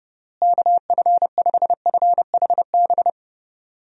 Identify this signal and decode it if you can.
Morse code, heard as a keyed tone, about 29 wpm, 700 Hz. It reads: KF5FHB
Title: USA Call Signs - 6 Characters - Repeated 3x - 30wpm